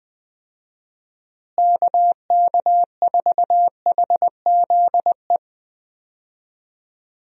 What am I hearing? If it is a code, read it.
Morse code, heard as a keyed tone, about 20 wpm, 700 Hz. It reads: KK4HZE
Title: USA Call Signs - 6 Characters - Repeated 2x - 20wpm